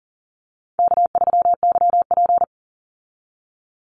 Morse code, heard as a keyed tone, about 40 wpm, 700 Hz. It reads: K3YP
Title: Easy Call Signs - 40wpm